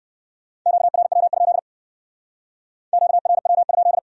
Morse code, heard as a keyed tone, about 56 wpm, 700 Hz. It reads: QRW? QRW?
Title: Speed-Racing Q-Codes - 45wpm